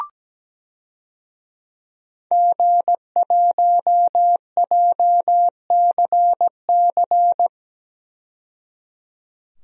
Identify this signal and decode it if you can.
Morse code, heard as a keyed tone, about 17 wpm, 700 Hz. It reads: G1JCC
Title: DXCC-100 Call Signs - 5 Characters - 17wpm